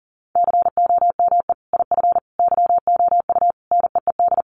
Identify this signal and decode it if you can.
Morse code, heard as a keyed tone, about 40 wpm, 700 Hz. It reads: COME IF YOU NEED
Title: Sentences from Top 1000 Words 40wpm